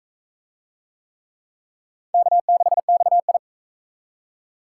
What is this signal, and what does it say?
Morse code, heard as a keyed tone, about 42 wpm, 700 Hz. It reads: K6XI